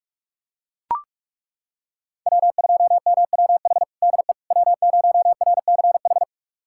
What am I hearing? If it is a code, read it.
Morse code, heard as a keyed tone, about 42 wpm, 700 Hz. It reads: W2MWH DE W0RCH
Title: USA Callsign DE Callsign - 5 Characters - 45wpm